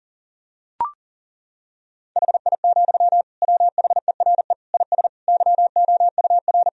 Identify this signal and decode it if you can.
Morse code, heard as a keyed tone, about 37 wpm, 700 Hz. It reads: HI, WHERE IS YOUR
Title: Sentences from Top 1000 Words - Review Part 3 - 40wpm